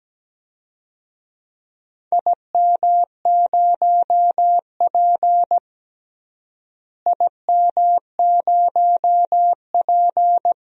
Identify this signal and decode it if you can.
Morse code, heard as a keyed tone, about 17 wpm, 700 Hz. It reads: IM0P IM0P
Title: DXCC-100 Call Signs - 4 Characters - Repeated 3x - 17wpm